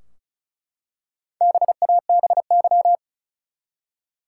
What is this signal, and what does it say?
Morse code, heard as a keyed tone, about 36 wpm, 700 Hz. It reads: BABY
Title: Four Letter Words - 35wpm